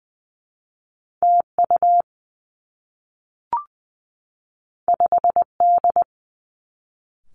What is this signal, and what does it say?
Morse code, heard as a keyed tone, about 20 wpm, 700 Hz. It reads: T U 5 D
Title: Two Challenging Letter-Number Combinations 20wpm